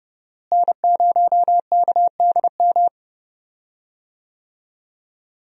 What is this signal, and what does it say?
Morse code, heard as a keyed tone, about 31 wpm, 700 Hz. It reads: N0KDM
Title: USA Call Signs - 5 Characters - Repeated 2x - 30wpm